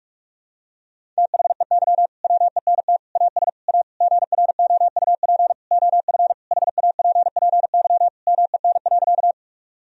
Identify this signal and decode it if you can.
Morse code, heard as a keyed tone, about 42 wpm, 700 Hz. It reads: THEY WENT AS A GROUP OF HAPPY MEN.
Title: Sentences from Top 1000 Words - Review Part 1 - 45wpm